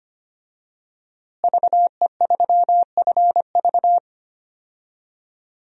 Morse code, heard as a keyed tone, about 25 wpm, 700 Hz. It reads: VE3FV
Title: DXCC-100 Call Signs - 5 Characters - Repeated 3x - 25wpm